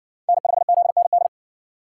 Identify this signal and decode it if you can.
Morse code, heard as a keyed tone, about 59 wpm, 700 Hz. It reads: N5ZND